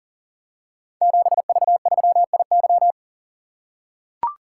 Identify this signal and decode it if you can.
Morse code, heard as a keyed tone, about 39 wpm, 700 Hz. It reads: 7V3IY
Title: Code Groups - 5 Letter:Numbers - 40wpm